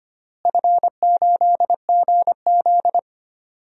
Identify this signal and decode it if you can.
Morse code, heard as a keyed tone, about 25 wpm, 700 Hz. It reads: F8GZ
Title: DXCC-100 Call Signs - 4 Characters - Repeated 2x - 25wpm